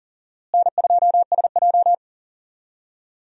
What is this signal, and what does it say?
Morse code, heard as a keyed tone, about 40 wpm, 700 Hz. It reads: N2SJ
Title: USA Call Signs - 4 Characters - Repeated 2x - 40wpm